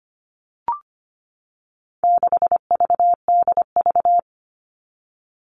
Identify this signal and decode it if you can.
Morse code, heard as a keyed tone, about 25 wpm, 700 Hz. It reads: 6VDV
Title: All Cut Numbers - 4-Digit 25wpm